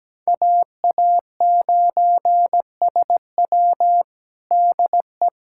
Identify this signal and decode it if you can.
Morse code, heard as a keyed tone, about 17 wpm, 700 Hz. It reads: AA9SW DE
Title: USA Callsign DE Callsign 17wpm